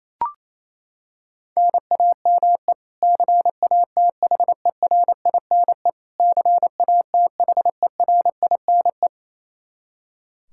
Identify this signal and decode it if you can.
Morse code, heard as a keyed tone, about 28 wpm, 700 Hz. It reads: NAME CATHERINE CATHERINE
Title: Practice QSO - Exchange 1 - Beginning - 28wpm